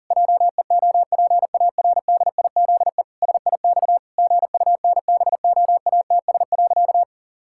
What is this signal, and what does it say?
Morse code, heard as a keyed tone, about 40 wpm, 700 Hz. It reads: JEOPARDIZE SIX GUNBOATS.
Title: Pangram Challenge v2 - 40wpm